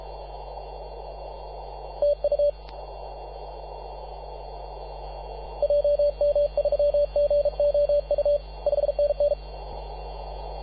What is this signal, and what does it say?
Morse code, heard as a keyed tone, about 36 wpm, 580 Hz, with radio noise. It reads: TU JM3GOU 5NN
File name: posts / 2015-sept-24-2118z-t2gc-18-071-cw